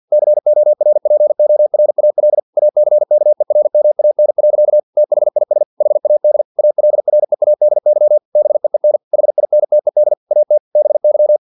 Morse code, heard as a keyed tone, about 50 wpm, 600 Hz. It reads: CORPORAL ACKERMAN. THIS HAD ALREADY BEEN HINTED AT BY
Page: Never 49wpm